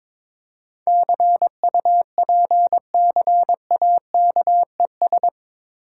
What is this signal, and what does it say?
Morse code, heard as a keyed tone, about 23 wpm, 690 Hz. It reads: CUPCAKES
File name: Difficult Words with Hints - 22wpm